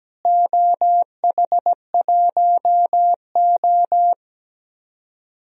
Morse code, heard as a keyed tone, about 17 wpm, 700 Hz. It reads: OH1O
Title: DXCC-100 Call Signs - 4 Characters - Repeated 3x - 17wpm